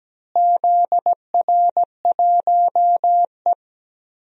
Morse code, heard as a keyed tone, about 17 wpm, 700 Hz. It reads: ZR1E
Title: DXCC-100 Call Signs - 4 Characters - Repeated 3x - 17wpm